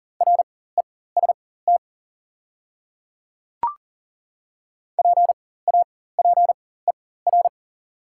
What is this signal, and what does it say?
Morse code, heard as a keyed tone, about 37 wpm, spaced as for 18 wpm, 690 Hz. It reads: REST PAPER
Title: 4x Character Spacing - Top 500 Words - 40wpm